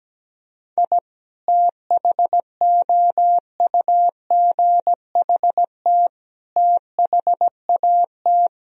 Morse code, heard as a keyed tone, about 17 wpm, 700 Hz. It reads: I THOUGHT THAT
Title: Sentences from Top 1000 Words - Review Part 4 - 17wpm